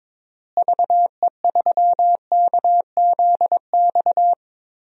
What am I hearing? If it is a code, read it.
Morse code, heard as a keyed tone, about 22 wpm, 700 Hz. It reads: VE3KZX
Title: DXCC-100 Call Signs - 6 Characters - Repeated 3x - 22wpm